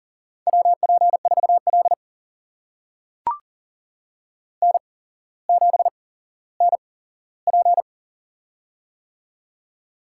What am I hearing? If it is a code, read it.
Morse code, heard as a keyed tone, about 40 wpm, spaced as for 17 wpm, 700 Hz. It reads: WP4L N7NP
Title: USA Call Signs - 4 Characters - 8x Character Spacing - 40wpm